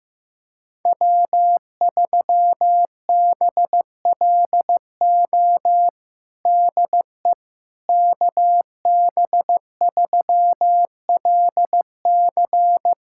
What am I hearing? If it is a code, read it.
Morse code, heard as a keyed tone, about 15 wpm, 700 Hz. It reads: W3BLO DE KB3LC
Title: USA Callsign DE Callsign - 5 Characters - 15wpm